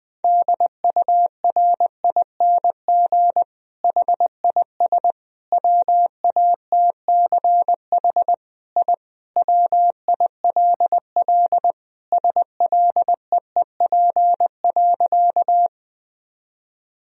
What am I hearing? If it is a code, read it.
Morse code, heard as a keyed tone, about 21 wpm, 700 Hz. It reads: DURING HIS WATCH I WILL SLEEP.
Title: Sentences from Top 1000 Words - Review Part 2 - 20wpm